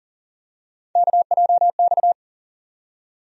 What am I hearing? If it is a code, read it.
Morse code, heard as a keyed tone, about 40 wpm, 700 Hz. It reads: KJX
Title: USA Call Sign Postfixes - 3 Letters 40wpm